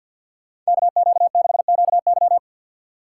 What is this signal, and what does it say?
Morse code, heard as a keyed tone, about 61 wpm, 700 Hz. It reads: KQ6QY
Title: Speed Racing - USA 2x2 Call Signs 50wpm